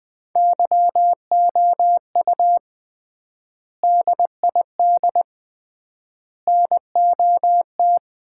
Morse code, heard as a keyed tone, about 20 wpm, 700 Hz. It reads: YOU DID NOT
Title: 3x Word-Spacing - Sets of 3 Words - 20wpm